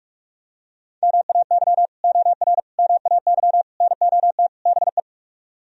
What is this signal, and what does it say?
Morse code, heard as a keyed tone, about 38 wpm, 700 Hz. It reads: MAY OR MAY NOT BE